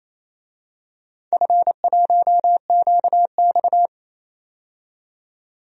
Morse code, heard as a keyed tone, about 28 wpm, 700 Hz. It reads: F1QX
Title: DXCC-100 Call Signs - 4 Characters - 28wpm